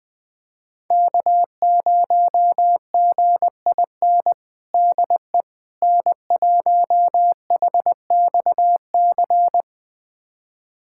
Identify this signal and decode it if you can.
Morse code, heard as a keyed tone, about 20 wpm, 700 Hz. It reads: K0GIN DE N1HXC
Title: USA Callsign DE Callsign - 5 Characters - 20wpm